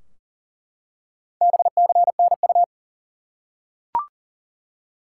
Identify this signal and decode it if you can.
Morse code, heard as a keyed tone, about 40 wpm, 700 Hz. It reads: BCNU
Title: Top 90 QSO Elements 40wpm